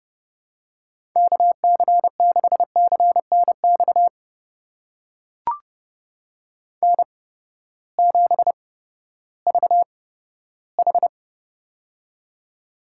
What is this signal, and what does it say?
Morse code, heard as a keyed tone, about 32 wpm, spaced as for 15 wpm, 700 Hz. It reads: KC6CNX N7VH
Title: USA Call Signs - 8x Character Spacing - 30wpm